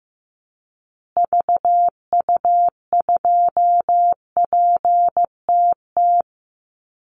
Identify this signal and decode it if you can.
Morse code, heard as a keyed tone, about 14 wpm, 700 Hz. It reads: VU2PTT
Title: Call Signs Repeated 2x - 15wpm